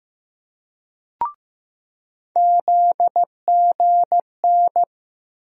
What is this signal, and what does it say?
Morse code, heard as a keyed tone, about 15 wpm, 700 Hz. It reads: ZGN
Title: USA Call Sign Postfixes - 3 Letters 15wpm